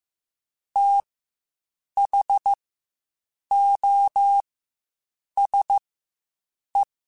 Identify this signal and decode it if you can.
Morse code, heard as a keyed tone, about 15 wpm, spaced as for 7 wpm, 800 Hz. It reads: THOSE
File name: Quotes Spaced / Quotes Spaced-mp3s / IgnoringFacts sp @10